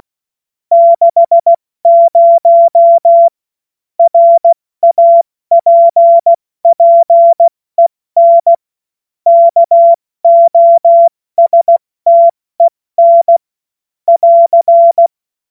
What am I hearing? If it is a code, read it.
Morse code, heard as a keyed tone, about 16 wpm, 680 Hz. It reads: 60 RAPPEN KOSTEN <AR>
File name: Morsekurs / Witze / Witz 15. November 2022 70 F80